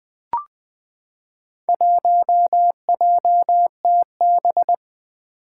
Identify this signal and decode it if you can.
Morse code, heard as a keyed tone, about 20 wpm, 700 Hz. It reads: 1JTB
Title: Four Challenging Letter-Number Combinations 20wpm